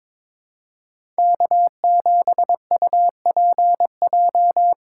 Morse code, heard as a keyed tone, about 21 wpm, 700 Hz. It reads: K7UPJ
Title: K1USN SST QSO Practice - 22wpm